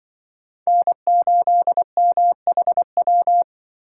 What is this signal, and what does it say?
Morse code, heard as a keyed tone, about 24 wpm, 700 Hz. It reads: N8MHW
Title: Speed Racing - USA 1x3 Call Signs 20wpm